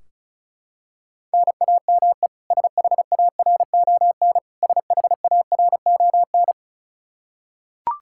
Morse code, heard as a keyed tone, about 36 wpm, 700 Hz. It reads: NAME SHARON SHARON
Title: K1USN SST QSO Practice 35wpm